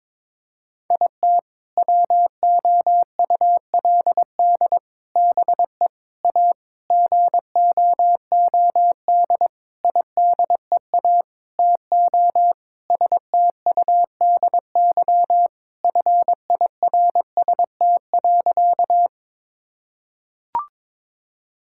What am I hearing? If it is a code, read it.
Morse code, heard as a keyed tone, about 22 wpm, 700 Hz. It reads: IT WOULD BE A GOOD IDEA TO STUDY FIRST.
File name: Sentences with 7-Word Phrases - 22wpm